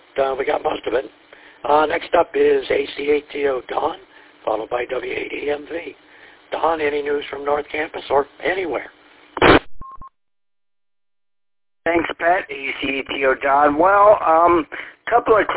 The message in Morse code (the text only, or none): none